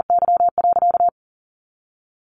Morse code, heard as a keyed tone, about 40 wpm, 700 Hz. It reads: Y.
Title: Sentences from Top 900 Words 40wpm